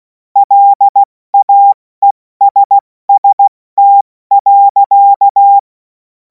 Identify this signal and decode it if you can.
Morse code, heard as a keyed tone, about 16 wpm, 800 Hz. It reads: LAESST.
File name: Morsekurs / Witze / Witz 24. März 2026 70 F80